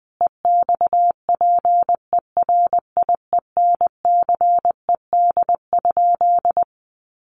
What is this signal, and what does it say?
Morse code, heard as a keyed tone, about 20 wpm, 700 Hz. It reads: EXPERIENCED?
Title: Sentences from Top 1100 Words 20wpm